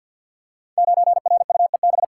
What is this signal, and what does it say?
Morse code, heard as a keyed tone, about 47 wpm, 700 Hz. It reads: 9RUEB
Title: Code Groups - 5 Letter:Numbers - 50wpm